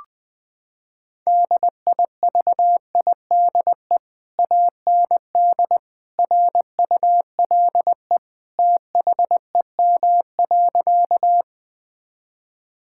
Morse code, heard as a keyed tone, about 20 wpm, 700 Hz. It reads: DIVIDE AND RULE THEM.
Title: Sentences from Top 1600 Words 20wpm